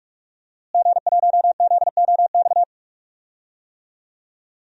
Morse code, heard as a keyed tone, about 43 wpm, 700 Hz. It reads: G1ZOX
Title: DXCC-100 Call Signs - 5 Characters - Repeated 3x - 45wpm